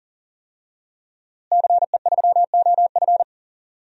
Morse code, heard as a keyed tone, about 37 wpm, 700 Hz. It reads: CE3OF
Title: DXCC-100 Call Signs - 5 Characters - Repeated 2x - 40wpm